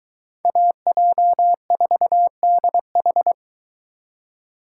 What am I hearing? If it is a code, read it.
Morse code, heard as a keyed tone, about 22 wpm, 700 Hz. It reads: AJ4DH